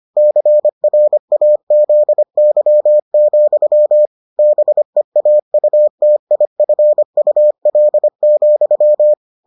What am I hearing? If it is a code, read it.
Morse code, heard as a keyed tone, about 25 wpm, 600 Hz. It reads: CRAZY, BEAUTIFUL,